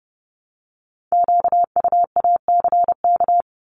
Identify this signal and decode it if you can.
Morse code, heard as a keyed tone, about 30 wpm, 700 Hz. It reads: QUACK
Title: Multi-letter Phonemes Repeated 3x 30wpm